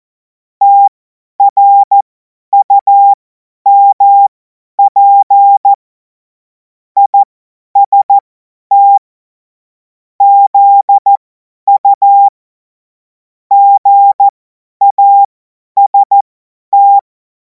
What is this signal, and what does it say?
Morse code, heard as a keyed tone, about 14 wpm, spaced as for 10 wpm, 800 Hz. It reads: TRUMP IST ZU GAST
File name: Morsekurs / Witze / Witz 12. Mai 2026 50 F70